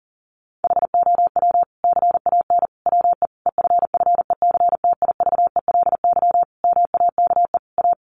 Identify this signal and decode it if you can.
Morse code, heard as a keyed tone, about 40 wpm, 700 Hz. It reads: HOW CAN WE EFFECTIVELY MAKE A
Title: Sentences from Top 1400 Words 40wpm